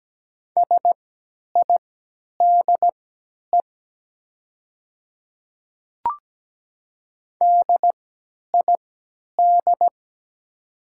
Morse code, heard as a keyed tone, about 17 wpm, spaced as for 10 wpm, 700 Hz. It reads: SIDE DID
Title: 3x Character Spacing - Top 100 Words - 17wpm